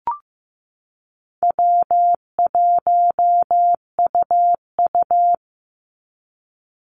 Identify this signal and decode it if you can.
Morse code, heard as a keyed tone, about 15 wpm, 700 Hz. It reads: W1UU
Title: Easy Call Signs Repeated 3x - 15wpm